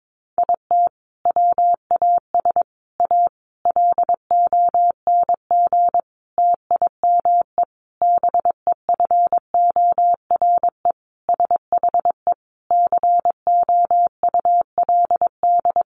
Morse code, heard as a keyed tone, about 22 wpm, 700 Hz. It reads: IT WAS A LONG TIME BEFORE SHE COULD
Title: Sentences from Top 500 Words 22wpm